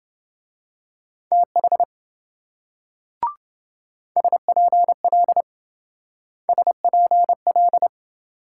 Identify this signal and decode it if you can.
Morse code, heard as a keyed tone, about 26 wpm, 700 Hz. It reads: TH SPL SPL